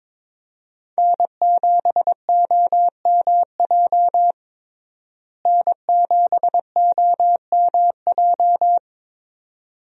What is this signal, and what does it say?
Morse code, heard as a keyed tone, about 22 wpm, 700 Hz. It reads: N7OMJ N7OMJ
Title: USA Call Signs - 5 Characters - Repeated 2x - 22wpm